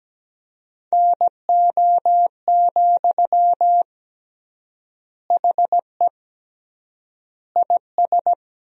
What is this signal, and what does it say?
Morse code, heard as a keyed tone, about 17 wpm, 700 Hz. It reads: NO, HE IS
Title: 3x Word-Spacing - Sentences from Top 400 Words - 17wpm